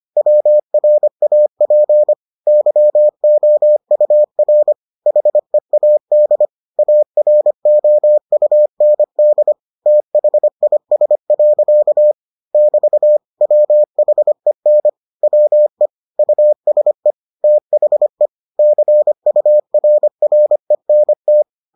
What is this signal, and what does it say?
Morse code, heard as a keyed tone, about 25 wpm, 600 Hz. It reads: WRAP YOUR HEAD AROUND THIS. <BT> WHEN WE USE THE CURRENT